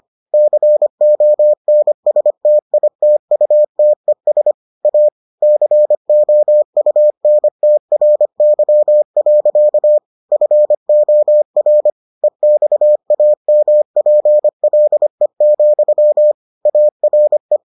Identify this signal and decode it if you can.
Morse code, heard as a keyed tone, about 25 wpm, 600 Hz. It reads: CONSTITUTES A COUNTRY. FOR EXAMPLE, ARE